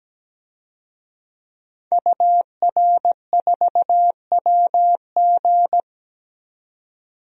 Morse code, heard as a keyed tone, about 17 wpm, 700 Hz. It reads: UR4WG
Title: DXCC-100 Call Signs - 5 Characters - 17wpm